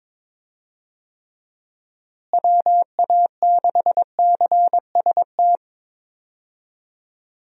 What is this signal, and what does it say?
Morse code, heard as a keyed tone, about 22 wpm, 700 Hz. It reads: WA6CST
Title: USA Call Signs - 6 Characters - Repeated 3x - 22wpm